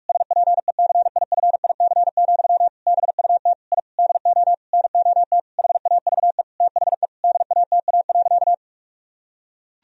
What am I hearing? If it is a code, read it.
Morse code, heard as a keyed tone, about 42 wpm, 700 Hz. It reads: SPECIFIC, BUT I DO NOT HAVE THE DATA.
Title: Sentences from Top 1000 Words - Review Part 3 - 45wpm